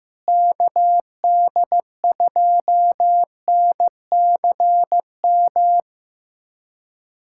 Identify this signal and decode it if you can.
Morse code, heard as a keyed tone, about 16 wpm, 700 Hz. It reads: KD2NCM
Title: USA Callsign DE Callsign - 6 Characters - 15wpm